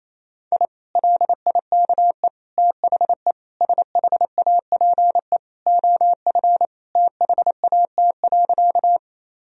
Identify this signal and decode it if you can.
Morse code, heard as a keyed tone, about 28 wpm, 700 Hz. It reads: I LIKE THE SHAPE OF THAT.
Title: Sentences from Top 1000 Words - Review Part 3 - 28wpm